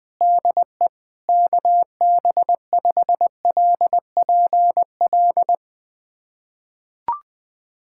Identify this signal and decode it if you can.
Morse code, heard as a keyed tone, about 20 wpm, 700 Hz. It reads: DE KB5LPL
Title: USA Callsign DE Callsign - 6 Characters - 20wpm